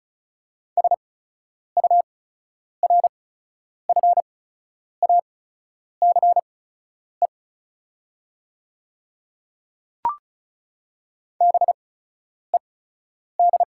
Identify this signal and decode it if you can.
Morse code, heard as a keyed tone, about 36 wpm, spaced as for 10 wpm, 700 Hz. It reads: SURFACE BED
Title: 8x Character Spacing - Top 500 Words - 35wpm